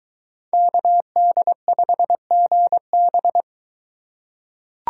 Morse code, heard as a keyed tone, about 23 wpm, 700 Hz. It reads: KD5GB